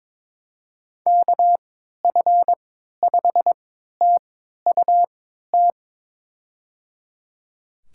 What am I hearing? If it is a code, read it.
Morse code, heard as a keyed tone, about 22 wpm, spaced as for 12 wpm, 700 Hz. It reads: KF5TUT
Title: USA Call Signs - 6 Characters - 3x Character Spacing - 22wpm